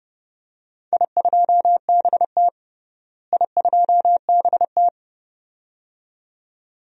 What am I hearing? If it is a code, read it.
Morse code, heard as a keyed tone, about 32 wpm, 700 Hz. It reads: I2BT I2BT
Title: DXCC-100 Call Signs - 4 Characters - Repeated 2x - 30wpm